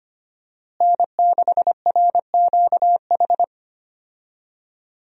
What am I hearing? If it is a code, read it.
Morse code, heard as a keyed tone, about 25 wpm, 700 Hz. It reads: N6RQH
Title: USA Call Signs - 5 Characters - Repeated 3x - 25wpm